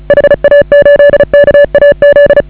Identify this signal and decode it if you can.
Morse code, heard as a keyed tone, about 34 wpm, 560 Hz. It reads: HA8KAZ
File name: AUDIO2019 / HA8KAZ CW